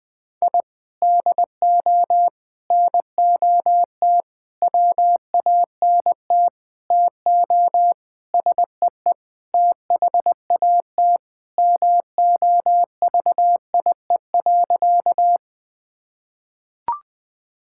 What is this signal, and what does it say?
Morse code, heard as a keyed tone, about 21 wpm, 700 Hz. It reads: I DO NOT WANT TO SEE THAT MOVIE.
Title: Sentences with 5-Word Phrases - Spoken Beginning - 20wpm